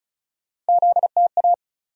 Morse code, heard as a keyed tone, about 36 wpm, 700 Hz. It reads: ZTA
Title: USA Call Sign Postfixes - 3 Letters 35wpm